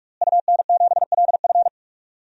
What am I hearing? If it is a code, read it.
Morse code, heard as a keyed tone, about 41 wpm, 700 Hz. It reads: UN7LF
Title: DXCC-100 Call Signs - 5 Characters - Repeated 3x - 45wpm